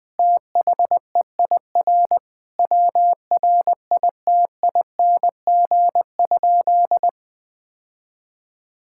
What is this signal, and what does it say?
Morse code, heard as a keyed tone, about 20 wpm, 700 Hz. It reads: THEIR WRITING?
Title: Sentences from Top 1000 Words - Review Part 6 - 20wpm